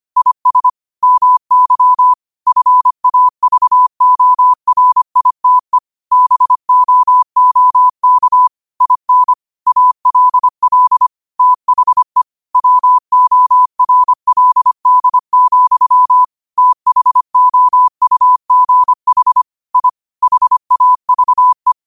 Morse code, heard as a keyed tone, about 25 wpm, 1000 Hz. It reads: IS MY FAVORITE BOOK IN ALL THE WORLD, THOUGH I HAVE